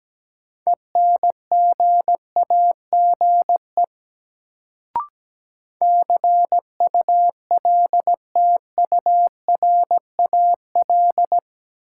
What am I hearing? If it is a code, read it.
Morse code, heard as a keyed tone, about 17 wpm, 700 Hz. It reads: ENGAGE CULTURAL
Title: Sentences from Top 1500 Words - New Word Drill - 17wpm